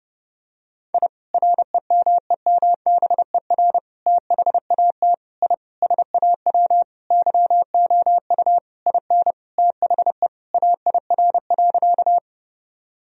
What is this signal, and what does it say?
Morse code, heard as a keyed tone, about 30 wpm, 700 Hz. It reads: I REMEMBER THAT I SAW YOU IN THE AIR.
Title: Sentences from Top 1000 Words - Review Part 1 - 30wpm